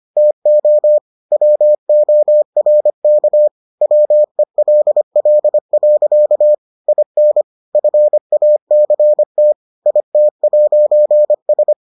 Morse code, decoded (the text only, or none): TO WORK WELL. IN FACT IT'S